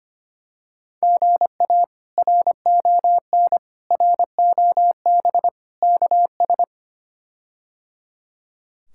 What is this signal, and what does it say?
Morse code, heard as a keyed tone, about 25 wpm, 700 Hz. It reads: GA RON ROB KS
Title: K1USN SST QSO Practice - 25wpm